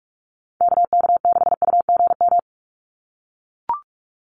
Morse code, heard as a keyed tone, about 44 wpm, 700 Hz. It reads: KK6UGM
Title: USA Call Signs 45wpm